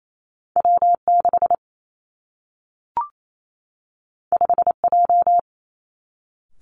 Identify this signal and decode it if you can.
Morse code, heard as a keyed tone, about 28 wpm, 700 Hz. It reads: W 6 5 J
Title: Two Challenging Letter-Number Combinations 28wpm